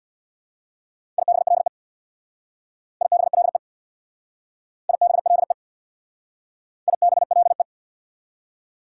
Morse code, heard as a keyed tone, about 67 wpm, 690 Hz. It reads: IBLE IBLE IBLE IBLE